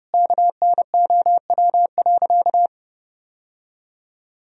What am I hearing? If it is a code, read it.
Morse code, heard as a keyed tone, about 30 wpm, 700 Hz. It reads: KNOW.
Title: Sentences from Top 1000 Words - Review Part 4 - 30wpm